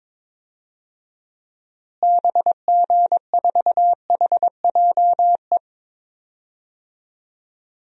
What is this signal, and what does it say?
Morse code, heard as a keyed tone, about 22 wpm, 700 Hz. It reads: BG4HJE
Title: DXCC-100 Call Signs - 6 Characters - Repeated 3x - 22wpm